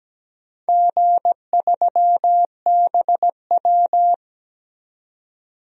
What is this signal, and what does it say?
Morse code, heard as a keyed tone, about 17 wpm, 700 Hz. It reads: G3BW
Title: DXCC-100 Call Signs - 4 Characters - 17wpm